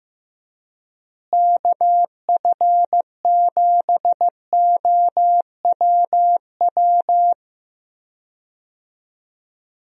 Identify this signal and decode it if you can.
Morse code, heard as a keyed tone, about 16 wpm, 700 Hz. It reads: KF7OWW
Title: USA Call Signs - 6 Characters - Repeated 3x - 15wpm